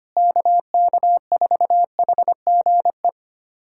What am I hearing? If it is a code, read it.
Morse code, heard as a keyed tone, about 25 wpm, 700 Hz. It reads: KK4HGE
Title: USA Callsign DE Callsign - 6 Characters - 25wpm